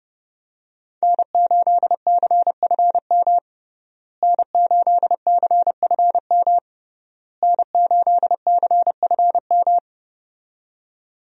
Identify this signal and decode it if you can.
Morse code, heard as a keyed tone, about 30 wpm, 700 Hz. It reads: N8CFM N8CFM N8CFM
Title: USA Call Signs - 5 Characters - Repeated 3x - 30wpm